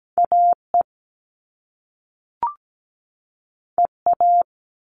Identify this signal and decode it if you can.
Morse code, heard as a keyed tone, about 17 wpm, 700 Hz. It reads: AE EA
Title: All Cut Numbers - 2-Digit 17wpm